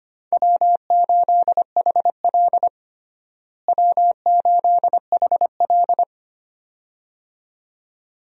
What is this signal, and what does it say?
Morse code, heard as a keyed tone, about 25 wpm, 700 Hz. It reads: W8HL W8HL
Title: USA Call Signs - 4 Characters - Repeated 2x - 25wpm